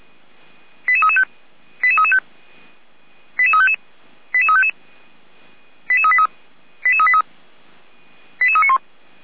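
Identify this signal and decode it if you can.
Five-tone selective call (ZVEI), heard as quick runs of five tones, tones of about 70 ms. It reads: 88335 88335 88355 88355 88383 88383 88391
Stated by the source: ZVEI five-tone selective call received on an SDR.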